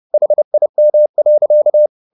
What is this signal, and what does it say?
Morse code, heard as a keyed tone, about 30 wpm, 600 Hz. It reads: HIM.